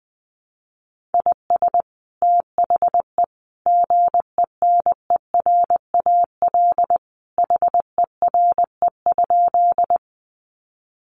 Morse code, heard as a keyed tone, about 19 wpm, 700 Hz. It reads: IS THE GENERAL HERE?
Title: Sentences from Top 600 Words 20wpm